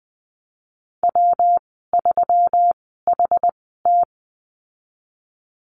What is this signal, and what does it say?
Morse code, heard as a keyed tone, about 20 wpm, spaced as for 15 wpm, 700 Hz. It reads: W3HT
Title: 128 Lesson 21g - Number 6 - Call Signs 20wpm